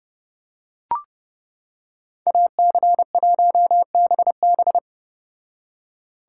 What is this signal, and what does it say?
Morse code, heard as a keyed tone, about 30 wpm, 700 Hz. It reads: AC1BB